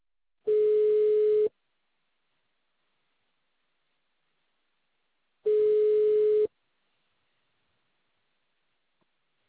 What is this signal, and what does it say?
Ringback tone (Europe, 425 Hz).